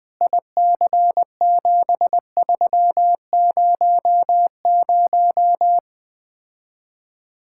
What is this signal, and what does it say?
Morse code, heard as a keyed tone, about 20 wpm, 700 Hz. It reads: IC7300
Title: Speed-Racing Common QSO Phrases - 17wpm